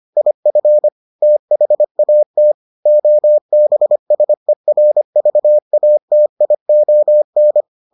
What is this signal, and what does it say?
Morse code, heard as a keyed tone, about 25 wpm, 600 Hz. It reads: IF THAT OBSERVATION